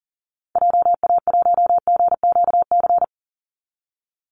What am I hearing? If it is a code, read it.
Morse code, heard as a keyed tone, about 40 wpm, 700 Hz. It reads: JA1GQC
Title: Call Signs - 40wpm